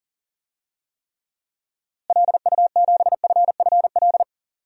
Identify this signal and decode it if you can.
Morse code, heard as a keyed tone, about 40 wpm, 700 Hz. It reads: LU7FFL
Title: DXCC-100 Call Signs - 6 Characters - Repeated 3x - 40wpm